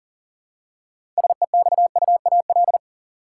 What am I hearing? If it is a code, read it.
Morse code, heard as a keyed tone, about 40 wpm, 700 Hz. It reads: SEXUAL
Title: Six Letter Words - 40wpm